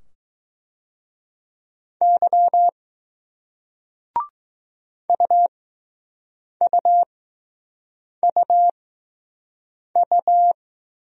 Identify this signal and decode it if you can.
Morse code, heard as a keyed tone, about 20 wpm, spaced as for 6 wpm, 700 Hz. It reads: Y UUUU